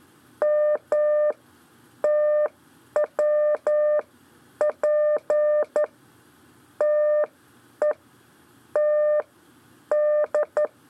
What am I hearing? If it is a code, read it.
Morse code, heard as a keyed tone, about 12 wpm, spaced as for 9 wpm, 600 Hz. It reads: M TWP TE T D
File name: 4 LEZIONE CW 1